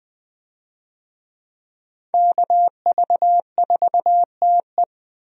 Morse code, heard as a keyed tone, about 20 wpm, 700 Hz. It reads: KV4TE
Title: USA Call Signs - 5 Characters - Repeated 2x - 20wpm